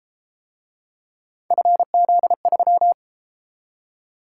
Morse code, heard as a keyed tone, about 32 wpm, 700 Hz. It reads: FZ3